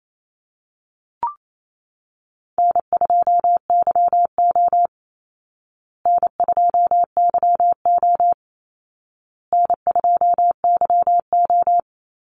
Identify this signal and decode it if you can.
Morse code, heard as a keyed tone, about 28 wpm, 700 Hz. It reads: N2YO N2YO N2YO
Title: Easy Call Signs Repeated 3x - 28wpm